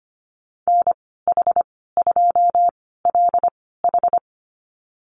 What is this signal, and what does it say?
Morse code, heard as a keyed tone, about 25 wpm, spaced as for 16 wpm, 690 Hz. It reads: NH2LH
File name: Lesson 15g - Number 5 - Call Signs 25wpm